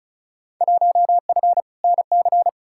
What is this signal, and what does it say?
Morse code, heard as a keyed tone, about 36 wpm, 700 Hz. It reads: 1F NC
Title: Field Day Exchanges - 35wpm